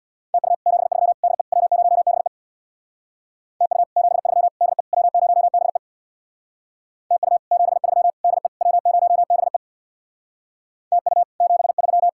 Morse code, heard as a keyed tone, about 61 wpm, 700 Hz. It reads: TU 73 DE W0BE TU 73 DE W0BE TU 73 DE W0BE TU 73
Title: Speed-Racing Common QSO Phrases - 50wpm